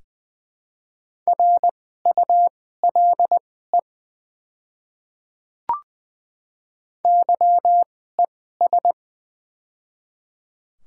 Morse code, heard as a keyed tone, about 20 wpm, spaced as for 14 wpm, 700 Hz. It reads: RULE YES
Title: 2x Character Spacing - Top 500 Words - 20wpm